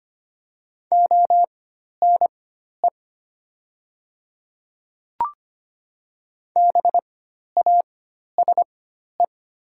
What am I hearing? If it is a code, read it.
Morse code, heard as a keyed tone, about 25 wpm, spaced as for 12 wpm, 700 Hz. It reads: ONE BASE
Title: 4x Character Spacing - Top 500 Words - 25wpm